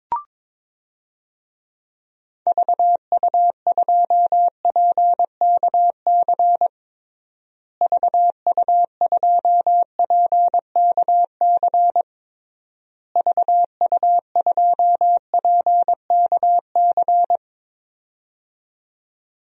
Morse code, heard as a keyed tone, about 22 wpm, 700 Hz. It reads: VU2PKC VU2PKC VU2PKC
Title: DXCC-100 Call Signs - 6 Characters - Repeated 3x - 22wpm